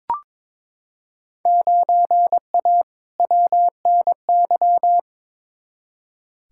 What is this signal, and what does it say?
Morse code, heard as a keyed tone, about 22 wpm, 700 Hz. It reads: 9A WNY
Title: Field Day Exchanges Encore - 22wpm